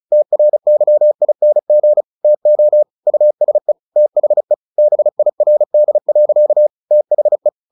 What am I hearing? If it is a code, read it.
Morse code, heard as a keyed tone, about 34 wpm, 600 Hz. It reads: TRYING TO USE THE BIRD. THE
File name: the old man and the sea / the old man and the sea 1 35wpm